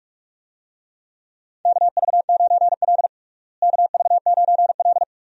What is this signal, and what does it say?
Morse code, heard as a keyed tone, about 47 wpm, 700 Hz. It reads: KV9L KV9L
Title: USA Call Signs - 4 Characters - Repeated 2x - 45wpm